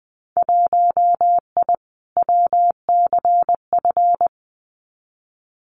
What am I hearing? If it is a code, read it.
Morse code, heard as a keyed tone, about 20 wpm, 700 Hz. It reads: 1I WCF
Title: Winter Field Day Exchanges Encore 20wpm